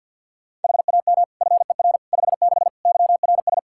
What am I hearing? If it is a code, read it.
Morse code, heard as a keyed tone, about 49 wpm, 700 Hz. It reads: HAM FER 56 YRS
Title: Practice QSO - Exchange 3 - More Info - 50wpm